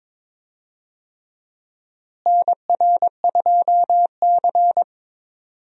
Morse code, heard as a keyed tone, about 22 wpm, 700 Hz. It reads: NR2C
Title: USA Call Signs - 4 Characters - Repeated 3x - 22wpm